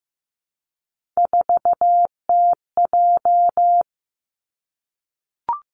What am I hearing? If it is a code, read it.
Morse code, heard as a keyed tone, about 15 wpm, 700 Hz. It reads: 4TJ
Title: Three Challenging Letter-Number Combinations 15wpm